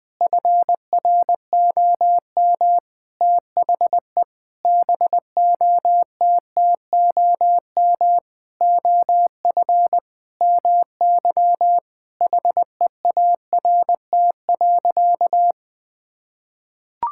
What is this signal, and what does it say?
Morse code, heard as a keyed tone, about 20 wpm, 700 Hz. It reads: FROM THE BOTTOM OF MY HEART.
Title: Sentences with 8-Word Phrases - Spoken Beginning - 20wpm